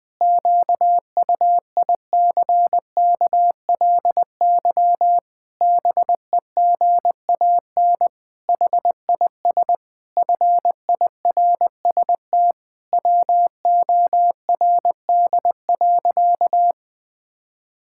Morse code, heard as a keyed tone, about 20 wpm, 700 Hz. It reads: QUICKLY BEGAN HIS FIRST WORD.
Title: Pangram Challenge v2 - 20wpm